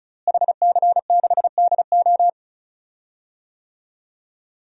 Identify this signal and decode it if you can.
Morse code, heard as a keyed tone, about 36 wpm, 700 Hz. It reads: HC6DO
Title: DXCC-100 Call Signs - 5 Characters - Repeated 3x - 35wpm